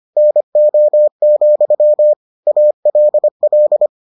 Morse code, heard as a keyed tone, about 25 wpm, 600 Hz. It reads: NO, ALL